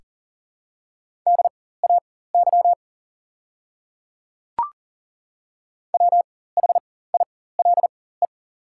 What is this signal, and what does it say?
Morse code, heard as a keyed tone, about 40 wpm, spaced as for 19 wpm, 690 Hz. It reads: DAY WHILE
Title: 4x Character Spacing - Top 300 Words - 40wpm